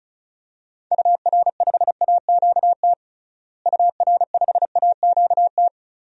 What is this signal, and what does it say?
Morse code, heard as a keyed tone, about 36 wpm, 700 Hz. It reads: UR5AQT UR5AQT
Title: DXCC-100 Call Signs - 6 Characters - Repeated 2x - 35wpm